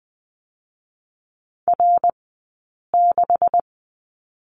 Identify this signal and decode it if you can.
Morse code, heard as a keyed tone, about 20 wpm, spaced as for 8 wpm, 700 Hz. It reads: R6
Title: 126 Lesson 21d - Number 6 - Two Characters 20wpm